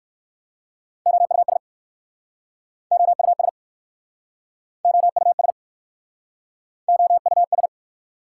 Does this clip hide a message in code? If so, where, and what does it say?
Yes: Morse code from 1.1 to 7.7 s, heard as a keyed tone, about 60 wpm, 690 Hz. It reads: OUS OUS OUS OUS